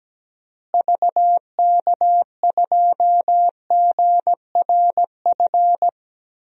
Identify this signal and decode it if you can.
Morse code, heard as a keyed tone, about 17 wpm, 700 Hz. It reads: VK2GRF
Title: DXCC-100 Call Signs - 6 Characters - Repeated 3x - 17wpm